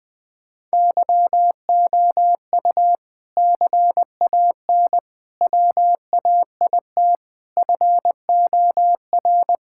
Morse code, heard as a keyed tone, about 20 wpm, 700 Hz. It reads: YOU CAN WAIT FOR
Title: Sentences from Top 1000 Words - Review Part 6 - 20wpm